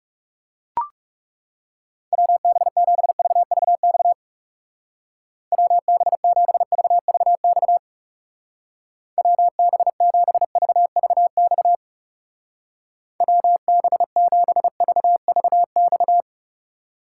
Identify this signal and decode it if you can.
Morse code, heard as a keyed tone, about 37 wpm, 700 Hz. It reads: WB7VVX WB7VVX WB7VVX WB7VVX
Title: Speed Racing - USA 2x3 Call Signs 30wpm